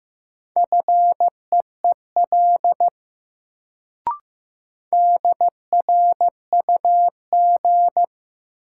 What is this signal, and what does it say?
Morse code, heard as a keyed tone, about 15 wpm, 700 Hz. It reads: FEEL DRUG
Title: Sets of 5 Words - New Word Drill - 15wpm